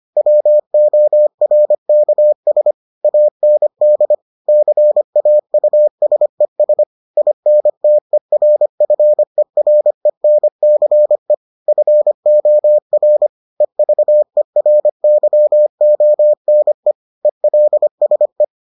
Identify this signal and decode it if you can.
Morse code, heard as a keyed tone, about 25 wpm, 600 Hz. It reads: WORKS AND CAUSES INTERFERENCE FOR EVERYONE ELSE